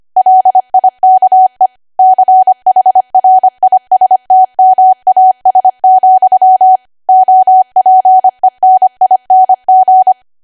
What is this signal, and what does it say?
Morse code, heard as a keyed tone, about 25 wpm, 750 Hz. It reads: LIKE CHRISTMAS, OPENING
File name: cwa / W1AW-Level3 / 25WPM